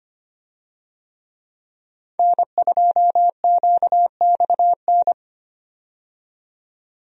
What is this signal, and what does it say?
Morse code, heard as a keyed tone, about 25 wpm, 700 Hz. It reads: N2QXN
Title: USA Call Signs - 5 Characters - 25wpm